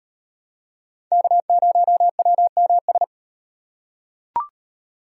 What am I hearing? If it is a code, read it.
Morse code, heard as a keyed tone, about 32 wpm, 700 Hz. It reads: K0WMS